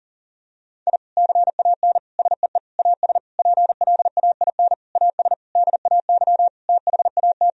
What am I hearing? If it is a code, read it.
Morse code, heard as a keyed tone, about 40 wpm, 700 Hz. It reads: I CAN SEE AS PLAIN AS DAY THAT
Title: Sentences from Top 1000 Words - Review Part 2 - 40wpm